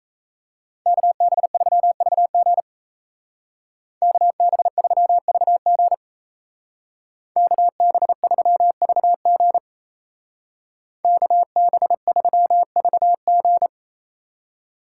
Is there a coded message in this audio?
Yes: Morse code from 0.9 to 13.7 s, heard as a keyed tone, about 37 wpm, 700 Hz. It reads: KB3VG KB3VG KB3VG KB3VG